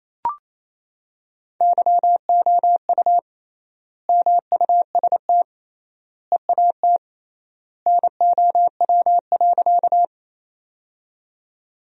Morse code, heard as a keyed tone, about 28 wpm, 700 Hz. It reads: YOU MUST EAT NOW.
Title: 3x Word-Spacing - Sentences from Top 300 Words - 28wpm